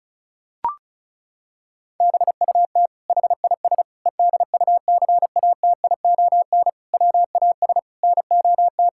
Morse code, heard as a keyed tone, about 34 wpm, 700 Hz. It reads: BUT HIS EDUCATION WAS NOT
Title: Sentences from Top 1000 Words - Review Part 6 - 35wpm